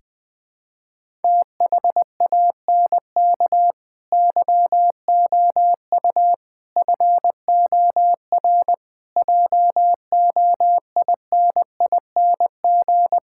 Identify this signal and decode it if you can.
Morse code, heard as a keyed tone, about 20 wpm, 700 Hz. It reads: THANK YOU FOR JOINING
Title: Sentences with 5-Word Phrases - 20wpm